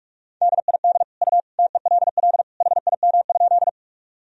Morse code, heard as a keyed tone, about 46 wpm, 700 Hz. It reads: DID U TELL HIM?
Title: Sentences from Top 200 Words and Abbreviations 45wpm